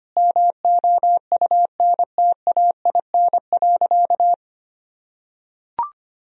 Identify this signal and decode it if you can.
Morse code, heard as a keyed tone, about 25 wpm, 700 Hz. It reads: MOUNTAIN.
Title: Sentences from Top 1000 Words - Review Part 2 - 25wpm